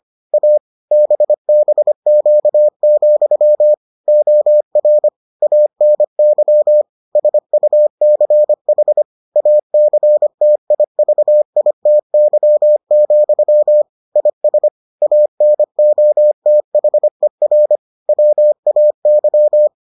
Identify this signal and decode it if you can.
Morse code, heard as a keyed tone, about 25 wpm, 600 Hz. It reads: A BBQ, OR ANY SUCH ACTIVITY, IS ANOTHER WAY